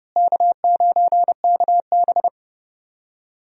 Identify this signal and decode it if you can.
Morse code, heard as a keyed tone, about 31 wpm, 700 Hz. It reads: K9KB
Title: USA Call Signs - 4 Characters - Repeated 3x - 30wpm